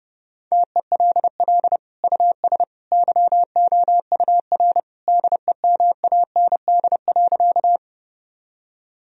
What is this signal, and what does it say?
Morse code, heard as a keyed tone, about 30 wpm, 700 Hz. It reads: TELL US YOUR DEMAND.
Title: Sentences from Top 1000 Words - Review Part 4 - 30wpm